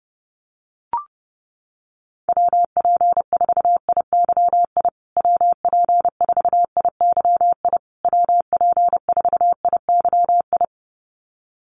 Morse code, heard as a keyed tone, about 30 wpm, 700 Hz. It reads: WP4IYI WP4IYI WP4IYI
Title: USA Call Signs Repeated 3x 30wpm